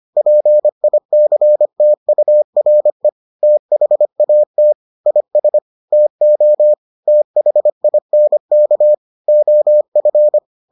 Morse code, heard as a keyed tone, about 25 wpm, 600 Hz. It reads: PICTURE THAT IS TO THINK OF